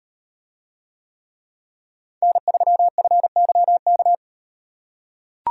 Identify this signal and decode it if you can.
Morse code, heard as a keyed tone, about 37 wpm, 700 Hz. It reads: N3FYK